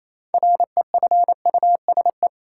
Morse code, heard as a keyed tone, about 28 wpm, 700 Hz. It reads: REFUSE
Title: Six Letter Words - 28wpm